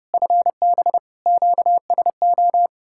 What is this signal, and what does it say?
Morse code, heard as a keyed tone, about 32 wpm, 700 Hz. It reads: FB QSO